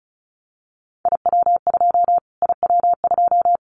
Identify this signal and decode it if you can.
Morse code, heard as a keyed tone, about 34 wpm, 700 Hz. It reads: IW2 IW2
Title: Callsign Prefixes for Top 25 DXCC - Repeated 3x - 35wpm